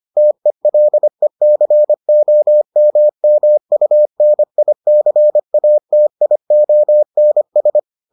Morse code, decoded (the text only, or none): TELECOMMUNICATIONS